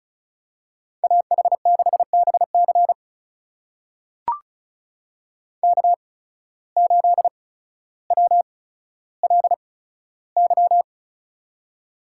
Morse code, heard as a keyed tone, about 36 wpm, spaced as for 15 wpm, 700 Hz. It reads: AH6BC K8WLY
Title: USA Call Signs - 5 Characters - 8x Character Spacing - 35wpm